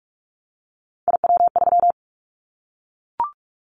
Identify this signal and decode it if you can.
Morse code, heard as a keyed tone, about 44 wpm, 700 Hz. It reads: IW3
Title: Callsign Prefixes for Top 25 DXCC - 45wpm